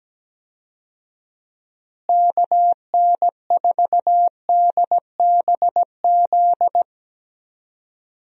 Morse code, heard as a keyed tone, about 17 wpm, 700 Hz. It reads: KN4DBZ
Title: USA Call Signs - 6 Characters - Repeated 2x - 17wpm